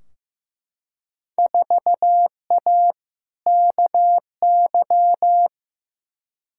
Morse code, heard as a keyed tone, about 15 wpm, 700 Hz. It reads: 4A KY
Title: Field Day Exchanges Encore - 15wpm